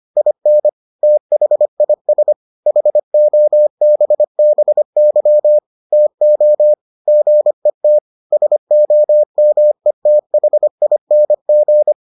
Morse code, heard as a keyed tone, about 25 wpm, 600 Hz. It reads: IN THIS HOBBY TO GET SOMETHING